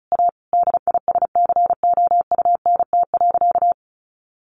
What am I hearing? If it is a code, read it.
Morse code, heard as a keyed tone, about 34 wpm, 700 Hz. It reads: A DISCOUNT.
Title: Sentences from Top 1200 Words 35wpm